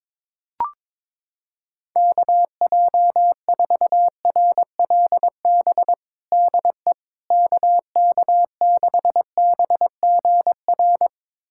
Morse code, heard as a keyed tone, about 22 wpm, 700 Hz. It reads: KJ4RLB DE KK6BGR
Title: USA Callsign DE Callsign - 6 Characters - 22wpm